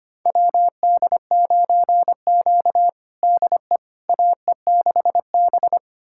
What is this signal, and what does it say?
Morse code, heard as a keyed tone, about 25 wpm, 700 Hz. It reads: WD9Q DE AE6B
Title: USA Callsign DE Callsign - 4 Characters - 25wpm